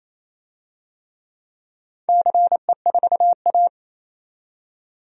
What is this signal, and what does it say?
Morse code, heard as a keyed tone, about 27 wpm, 690 Hz. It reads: CE4A
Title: DXCC-100 Call Signs - 4 Characters - 28wpm